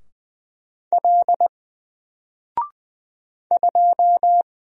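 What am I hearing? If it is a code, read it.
Morse code, heard as a keyed tone, about 20 wpm, 700 Hz. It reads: L 2
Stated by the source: Single letter and number Morse drill at 20 wpm, from N4LCD.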